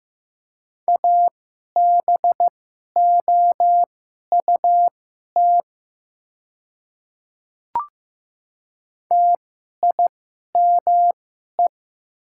Morse code, heard as a keyed tone, about 15 wpm, spaced as for 11 wpm, 700 Hz. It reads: ABOUT TIME
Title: 2x Character Spacing - Top 100 Words - 15wpm